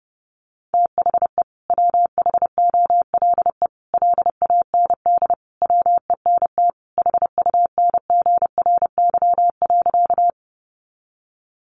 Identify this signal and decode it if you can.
Morse code, heard as a keyed tone, about 30 wpm, 700 Hz. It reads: THE WHOLE LAND WENT HUNGRY.
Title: Sentences from Top 1400 Words 30wpm